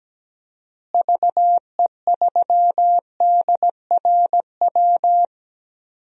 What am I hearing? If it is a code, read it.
Morse code, heard as a keyed tone, about 17 wpm, 700 Hz. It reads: VE3DRW
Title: DXCC-100 Call Signs - 6 Characters - Repeated 3x - 17wpm